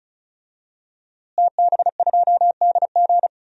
Morse code, heard as a keyed tone, about 34 wpm, 700 Hz. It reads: TB2DG
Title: DXCC-100 Call Signs - 5 Characters - Repeated 2x - 35wpm